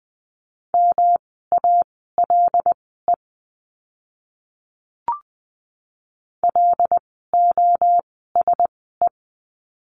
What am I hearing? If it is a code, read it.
Morse code, heard as a keyed tone, about 20 wpm, spaced as for 14 wpm, 700 Hz. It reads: MALE LOSE
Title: 108 Lesson 18f - Letter m - Words 20wpm